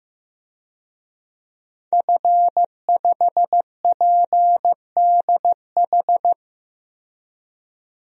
Morse code, heard as a keyed tone, about 15 wpm, 700 Hz. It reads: F5PDH
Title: DXCC-100 Call Signs - 5 Characters - 15wpm